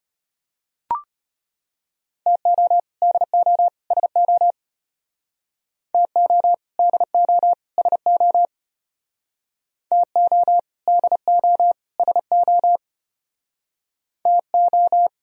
Morse code, heard as a keyed tone, about 33 wpm, 700 Hz. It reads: TO DO SO TO DO SO TO DO SO TO